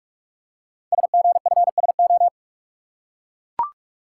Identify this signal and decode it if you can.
Morse code, heard as a keyed tone, about 44 wpm, 690 Hz. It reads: SGFSO
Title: Code Groups - 5 Letters - 45wpm